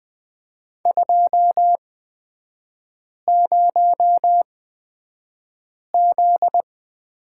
Morse code, heard as a keyed tone, about 20 wpm, spaced as for 5 wpm, 700 Hz. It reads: 20Z
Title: 255 Lesson 41e - Period - Three Characters 20wpm